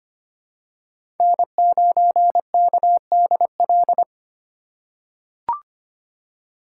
Morse code, heard as a keyed tone, about 25 wpm, 700 Hz. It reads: N9KDL
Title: USA Call Signs - 2x Character Spacing - 25wpm